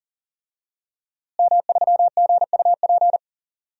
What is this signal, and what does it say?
Morse code, heard as a keyed tone, about 40 wpm, 700 Hz. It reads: M3GUP
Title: DXCC-100 Call Signs - 5 Characters - Repeated 2x - 40wpm